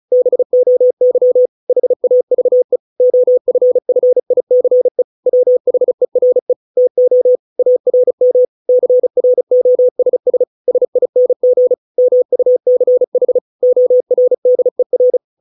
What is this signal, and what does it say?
Morse code, heard as a keyed tone, about 34 wpm, 500 Hz. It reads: BOY HAVE OFFICE WHERE TO ARM CROSS SING MUCH ORDER